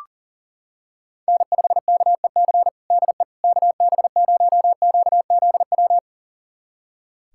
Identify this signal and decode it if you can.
Morse code, heard as a keyed tone, about 40 wpm, 700 Hz. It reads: N5KEC DE KB0QZW
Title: USA Callsign DE Callsign 40wpm